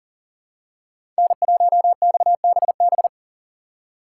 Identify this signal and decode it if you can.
Morse code, heard as a keyed tone, about 40 wpm, 700 Hz. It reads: N1XBB
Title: USA Call Signs - 5 Characters - 6x Character Spacing - 40wpm